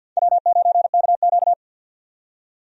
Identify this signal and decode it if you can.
Morse code, heard as a keyed tone, about 51 wpm, 700 Hz. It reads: W9KQ
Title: USA Call Signs - 4 Characters - 50wpm